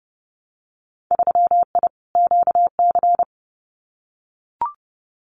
Morse code, heard as a keyed tone, about 30 wpm, 700 Hz. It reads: 3I QC
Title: Winter Field Day Exchanges Encore 30wpm